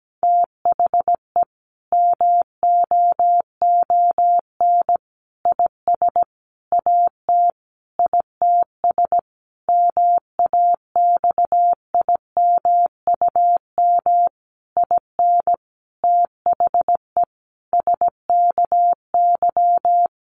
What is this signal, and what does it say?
Morse code, heard as a keyed tone, about 17 wpm, 700 Hz. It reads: THE MOON IS AT ITS MAXIMUM IN THE SKY
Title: Sentences from Top 1300 Words 17wpm